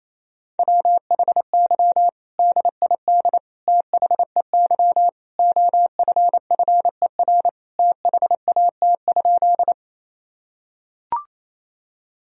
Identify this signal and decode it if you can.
Morse code, heard as a keyed tone, about 28 wpm, 700 Hz. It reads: WHY DID THEY OFFER THAT?
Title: Sentences from Top 1000 Words - Review Part 4 - 28wpm